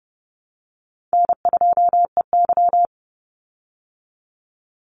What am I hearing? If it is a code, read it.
Morse code, heard as a keyed tone, about 30 wpm, 700 Hz. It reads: N2EY
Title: Call Signs Repeated 3x - 30wpm